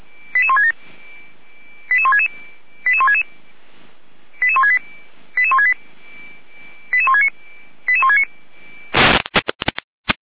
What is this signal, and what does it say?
Five-tone selective call (ZVEI), heard as quick runs of five tones, tones of about 70 ms. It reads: 88167 88166 88166 88168 88169 88169